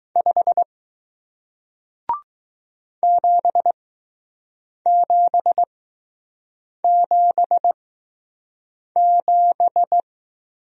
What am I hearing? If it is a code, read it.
Morse code, heard as a keyed tone, about 20 wpm, spaced as for 6 wpm, 700 Hz. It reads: 5 7777